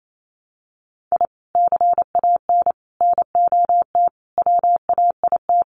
Morse code, heard as a keyed tone, about 28 wpm, 700 Hz. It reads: I CAN NOT WAIT
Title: Sets of 5 Words Encore 28wpm